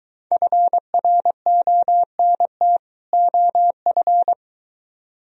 Morse code, heard as a keyed tone, about 23 wpm, 700 Hz. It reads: FRONT OF